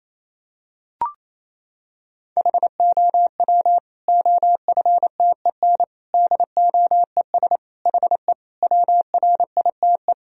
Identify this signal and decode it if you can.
Morse code, heard as a keyed tone, about 28 wpm, 700 Hz. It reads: HOW OFTEN DOES HE WRITE
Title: Sentences from Top 1000 Words - Review Part 1 - 28wpm